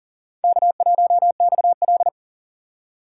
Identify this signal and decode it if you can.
Morse code, heard as a keyed tone, about 40 wpm, 700 Hz. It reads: K1XL
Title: USA Call Signs - 4 Characters - 6x Character Spacing - 40wpm